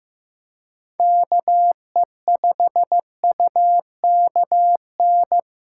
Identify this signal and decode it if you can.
Morse code, heard as a keyed tone, about 16 wpm, 700 Hz. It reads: KE5UKN
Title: USA Callsign DE Callsign 15wpm